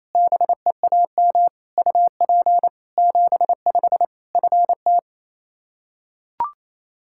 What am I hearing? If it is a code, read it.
Morse code, heard as a keyed tone, about 28 wpm, 700 Hz. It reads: BEAM UP 75 FT
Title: Practice QSO - Exchange 2 - Info - 28wpm